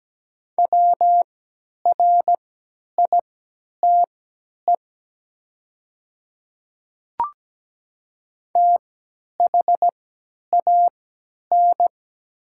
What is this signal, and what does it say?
Morse code, heard as a keyed tone, about 17 wpm, spaced as for 10 wpm, 700 Hz. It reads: WRITE THAN
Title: 3x Character Spacing - Top 100 Words - 17wpm